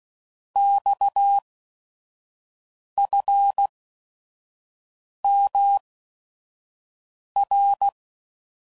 Morse code, heard as a keyed tone, about 16 wpm, spaced as for 5 wpm, 800 Hz. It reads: XFMR
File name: Mixed / Abreviations / Abreviations